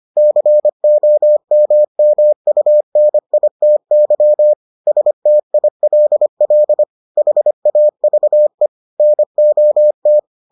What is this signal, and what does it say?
Morse code, heard as a keyed tone, about 25 wpm, 600 Hz. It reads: COMMUNITY STILL HAVE NOT